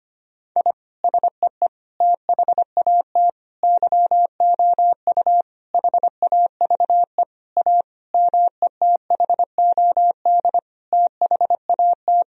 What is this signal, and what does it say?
Morse code, heard as a keyed tone, about 25 wpm, 700 Hz. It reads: I SEE THAT YOU HAVE A METHOD THAT
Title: Sentences from Top 1000 Words - Review Part 3 - 25wpm